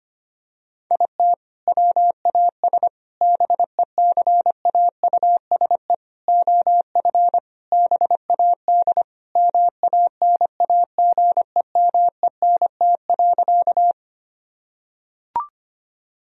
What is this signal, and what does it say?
Morse code, heard as a keyed tone, about 25 wpm, 690 Hz. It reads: IT WAS BECAUSE OF BAD MANAGEMENT.
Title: Sentences from Top 1000 Words - Review Part 4 - 25wpm